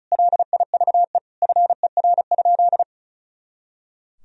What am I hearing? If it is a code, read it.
Morse code, heard as a keyed tone, about 36 wpm, 700 Hz. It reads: LIVE FER?
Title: Sentences from Top 200 Words and Abbreviations 35wpm